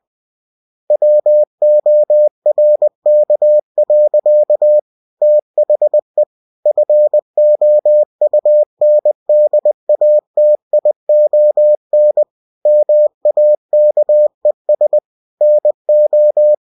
Morse code, heard as a keyed tone, about 20 wpm, 600 Hz. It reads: WORK. THE FOUNDATION MAKES NO
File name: cw-books / winnie-the-pooh / winnie-20wpm-page-0114